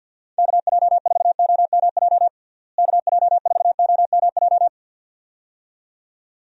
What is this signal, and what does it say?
Morse code, heard as a keyed tone, about 47 wpm, 700 Hz. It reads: KJ4OMJ KJ4OMJ
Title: USA Call Signs - 6 Characters - Repeated 2x - 50wpm